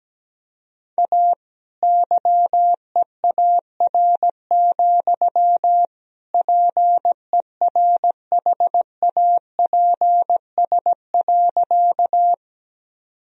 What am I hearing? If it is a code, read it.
Morse code, heard as a keyed tone, about 17 wpm, 700 Hz. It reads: A YEAR, PERHAPS.
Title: Sentences from Top 1000 Words - Review Part 1 - 17wpm